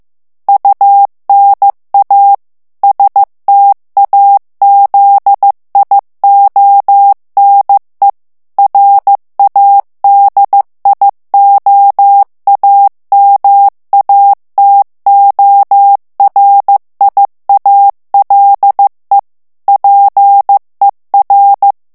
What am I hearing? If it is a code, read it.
Morse code, heard as a keyed tone, about 15 wpm, 800 Hz. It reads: UNA STAZIONE RADIOAMATORIALE PER
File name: ITALIEN texte clair 1 vitesse 14 mots minute